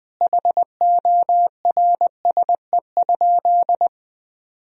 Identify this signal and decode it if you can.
Morse code, heard as a keyed tone, about 20 wpm, 700 Hz. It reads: HORSE?
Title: Sentences from Top 1000 Words - Review Part 2 - 20wpm